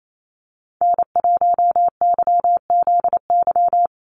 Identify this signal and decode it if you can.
Morse code, heard as a keyed tone, about 28 wpm, 700 Hz. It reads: N1YZY
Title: USA Call Signs 28wpm